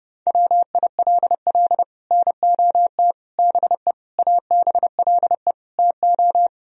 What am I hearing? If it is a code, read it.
Morse code, heard as a keyed tone, about 28 wpm, 700 Hz. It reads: WILL NOT BE ABLE TO